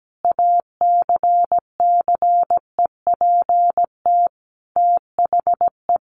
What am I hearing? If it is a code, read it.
Morse code, heard as a keyed tone, about 17 wpm, 700 Hz. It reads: ACCEPT THE
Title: Sentences from Top 1300 Words 17wpm